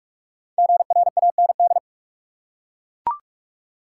Morse code, heard as a keyed tone, about 42 wpm, 700 Hz. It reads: GRAND
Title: Five Letter Words - 45wpm